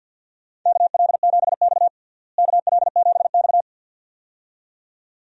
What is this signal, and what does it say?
Morse code, heard as a keyed tone, about 53 wpm, 700 Hz. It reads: KL7X KL7X
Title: DXCC-100 Call Signs - 4 Characters - Repeated 2x - 50wpm